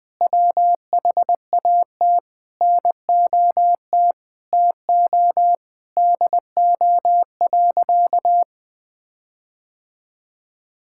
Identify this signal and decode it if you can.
Morse code, heard as a keyed tone, about 20 wpm, 700 Hz. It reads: WHAT NOT TO DO.
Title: Sentences with 8-Word Phrases - Spoken Beginning - 20wpm